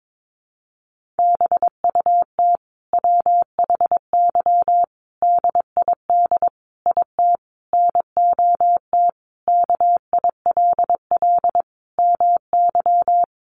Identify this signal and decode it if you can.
Morse code, heard as a keyed tone, about 22 wpm, 700 Hz. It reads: BUT WHY DID IT NOT KILL MY
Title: Sentences from Top 1400 Words 22wpm